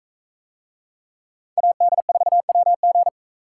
Morse code, heard as a keyed tone, about 40 wpm, 700 Hz. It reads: AD4WG